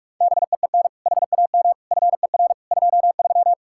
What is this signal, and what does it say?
Morse code, heard as a keyed tone, about 58 wpm, spaced as for 51 wpm, 700 Hz. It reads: BEEN HAM FER 23
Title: Speed-Racing Common QSO Phrases Encore - 45wpm